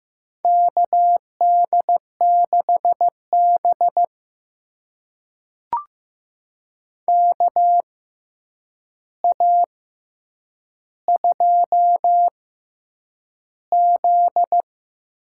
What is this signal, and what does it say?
Morse code, heard as a keyed tone, about 15 wpm, spaced as for 8 wpm, 700 Hz. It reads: KD6B KA2Z
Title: USA Call Signs - 6x Character Spacing - 15wpm